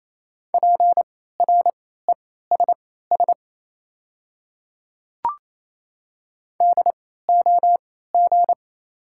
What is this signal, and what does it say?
Morse code, heard as a keyed tone, about 28 wpm, spaced as for 16 wpm, 690 Hz. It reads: PRESS DOG
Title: 3x Character Spacing - Top 500 Words - 28wpm